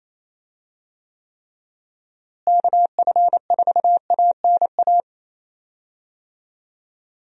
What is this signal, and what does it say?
Morse code, heard as a keyed tone, about 28 wpm, 700 Hz. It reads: KF4ANA
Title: USA Call Signs - 6 Characters - Repeated 3x - 28wpm